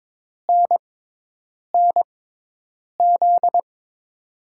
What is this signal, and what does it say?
Morse code, heard as a keyed tone, about 22 wpm, spaced as for 8 wpm, 700 Hz. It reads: NNZ
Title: USA Call Signs - 5 Characters - 6x Character Spacing - 22wpm